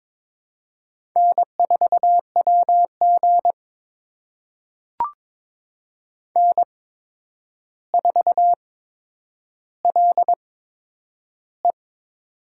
Morse code, heard as a keyed tone, about 22 wpm, spaced as for 9 wpm, 700 Hz. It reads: N4WG N4LE
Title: USA Call Signs - 4 Characters - 8x Character Spacing - 22wpm